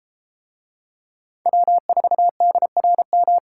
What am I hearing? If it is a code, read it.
Morse code, heard as a keyed tone, about 33 wpm, 700 Hz. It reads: W4DRM